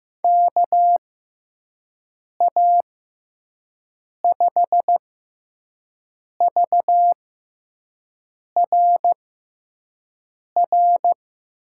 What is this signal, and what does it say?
Morse code, heard as a keyed tone, about 15 wpm, spaced as for 5 wpm, 700 Hz. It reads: KA5VRR
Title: USA Call Signs - 6 Characters - 6x Character Spacing - 15wpm